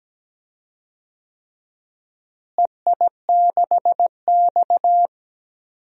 Morse code, heard as a keyed tone, about 17 wpm, 700 Hz. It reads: EI6X
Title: DXCC-100 Call Signs - 4 Characters - Repeated 3x - 17wpm